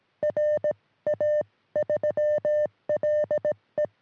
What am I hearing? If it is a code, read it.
Morse code, heard as a keyed tone, about 18 wpm, spaced as for 15 wpm, 600 Hz. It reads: RA3LE
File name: sm7lcb / 1943 ra3le 23cm